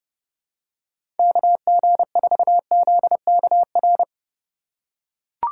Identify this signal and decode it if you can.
Morse code, heard as a keyed tone, about 30 wpm, 700 Hz. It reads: KG4ZKR
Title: USA Call Signs - 6 Characters - 2x Character Spacing - 30wpm